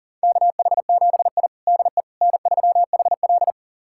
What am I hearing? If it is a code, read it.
Morse code, heard as a keyed tone, about 40 wpm, 700 Hz. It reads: KH7I DE N3HL
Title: USA Callsign DE Callsign - 4 Characters - 40wpm